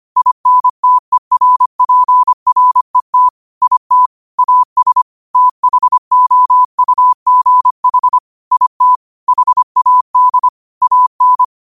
Morse code, heard as a keyed tone, about 25 wpm, 1000 Hz. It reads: INTERPRET IT AS THOUGH IT HAD AN
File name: 25wpm